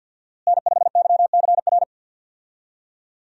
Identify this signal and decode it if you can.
Morse code, heard as a keyed tone, about 46 wpm, 700 Hz. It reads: N5YCR
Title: USA Call Signs - 5 Characters - Repeated 2x - 50wpm